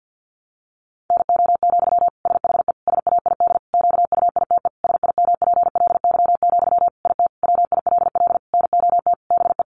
Morse code, heard as a keyed tone, about 50 wpm, 700 Hz. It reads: NO, SHE SAID QUITE SIMPLY, IT WILL NOT BE
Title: Sentences from Top 800 Words 50wpm